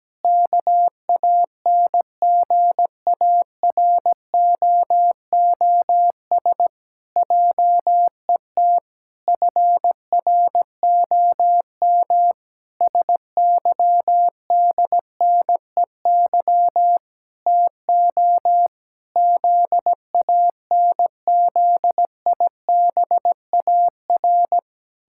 Morse code, heard as a keyed tone, about 17 wpm, 700 Hz. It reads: KANGAROOS JET FROM SYDNEY TO ZANZIBAR
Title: Pangram Challenge v2 - 17wpm